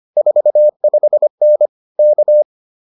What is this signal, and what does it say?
Morse code, heard as a keyed tone, about 25 wpm, 600 Hz. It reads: 45N K